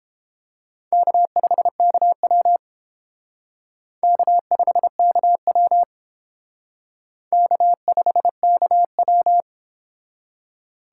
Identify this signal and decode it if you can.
Morse code, heard as a keyed tone, about 28 wpm, 700 Hz. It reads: K5KW K5KW K5KW